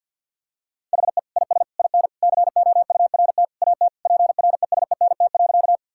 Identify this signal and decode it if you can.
Morse code, heard as a keyed tone, about 50 wpm, 700 Hz. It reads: HE IS IN COURT AT PRESENT.
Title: Sentences from Top 1600 Words 50wpm